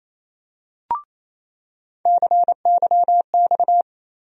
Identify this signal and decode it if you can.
Morse code, heard as a keyed tone, about 28 wpm, 700 Hz. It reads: CYX
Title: USA Call Sign Postfixes - 3 Letters 28wpm